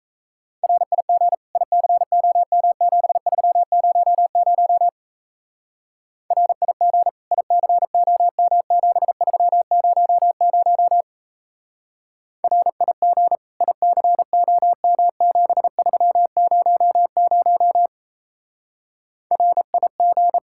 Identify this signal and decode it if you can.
Morse code, heard as a keyed tone, about 35 wpm, 700 Hz. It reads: RIG ICOM7300 RIG ICOM7300 RIG ICOM7300 RIG